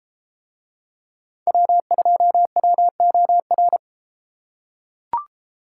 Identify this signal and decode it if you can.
Morse code, heard as a keyed tone, about 32 wpm, 700 Hz. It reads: W2WOR